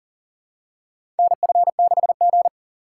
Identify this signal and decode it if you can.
Morse code, heard as a keyed tone, about 38 wpm, 700 Hz. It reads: NF6G
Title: USA Call Signs - 4 Characters - 6x Character Spacing - 40wpm